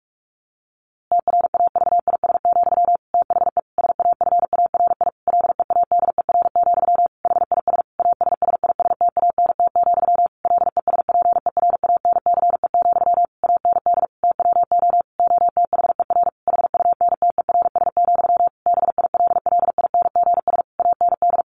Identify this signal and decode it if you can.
Morse code, heard as a keyed tone, about 44 wpm, 700 Hz. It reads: TRAVIS, THE SAFARI LEADER, HIS ASSISTANT, LESPERANCE, AND TWO OTHER HUNTERS, BILLINGS AND
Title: A Sound Of Thunder 45wpm